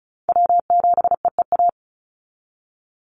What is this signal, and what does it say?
Morse code, heard as a keyed tone, about 36 wpm, 700 Hz. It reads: W7EEA
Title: USA Call Signs 35wpm